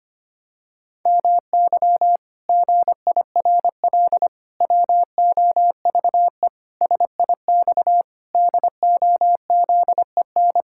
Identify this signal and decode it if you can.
Morse code, heard as a keyed tone, about 25 wpm, 700 Hz. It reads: MY GIRL WOVE SIX DOZEN
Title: Pangram Challenge v2 - 25wpm